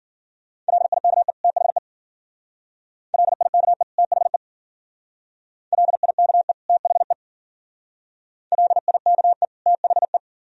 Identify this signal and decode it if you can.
Morse code, heard as a keyed tone, about 50 wpm, 690 Hz. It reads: LIKE THE LIKE THE LIKE THE LIKE THE
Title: Speed-Racing - Sets of 2 Words 40wpm